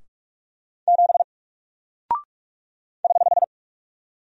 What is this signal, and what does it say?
Morse code, heard as a keyed tone, about 44 wpm, 700 Hz. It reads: 7 <HH>
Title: Single Character (Letter:Number:Punctuation:Prosign) Alt 45wpm